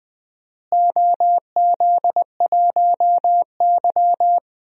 Morse code, heard as a keyed tone, about 20 wpm, 700 Hz. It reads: OZ1Y
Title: DXCC-100 Call Signs - 4 Characters - Repeated 2x - 20wpm